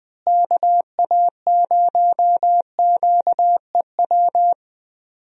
Morse code, heard as a keyed tone, about 20 wpm, 700 Hz. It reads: KA0QEW
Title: USA Call Signs - 6 Characters - Repeated 2x - 20wpm